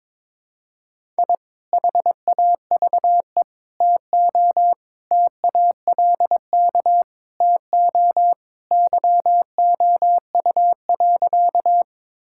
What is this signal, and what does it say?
Morse code, heard as a keyed tone, about 21 wpm, 700 Hz. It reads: I HAVE TO TALK TO YOU.
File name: practice / 22wpm / Sentences with 3-Word Phrases - 22wpm